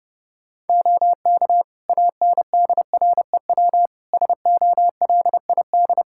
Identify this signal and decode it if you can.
Morse code, heard as a keyed tone, about 32 wpm, 700 Hz. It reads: OK ANDREW SOLID
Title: Practice QSO - Exchange 3 - More Info - 30wpm